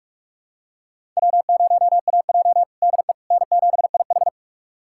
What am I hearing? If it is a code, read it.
Morse code, heard as a keyed tone, about 46 wpm, 700 Hz. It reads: W0AJ DE N7IH
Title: USA Callsign DE Callsign - 4 Characters - 45wpm